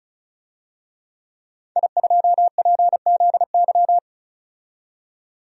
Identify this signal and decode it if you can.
Morse code, heard as a keyed tone, about 36 wpm, 700 Hz. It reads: I2PZY
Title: DXCC-100 Call Signs - 5 Characters - 35wpm